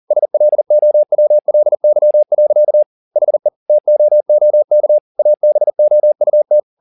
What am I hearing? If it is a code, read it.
Morse code, heard as a keyed tone, about 40 wpm, 600 Hz. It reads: SLOWLY. HE TOOK ABOUT
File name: the old man and the sea 2 40wpm